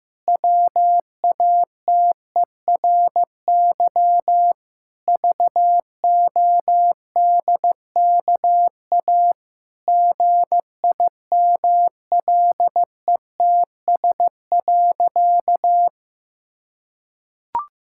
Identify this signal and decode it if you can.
Morse code, heard as a keyed tone, about 15 wpm, 700 Hz. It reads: WATERY VODKA GIMLETS.
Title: Pangram Challenge v2 - 15wpm